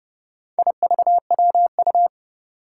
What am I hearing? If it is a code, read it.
Morse code, heard as a keyed tone, about 30 wpm, 700 Hz. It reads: IVWU
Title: Four Challenging Letter-Number Combinations 30wpm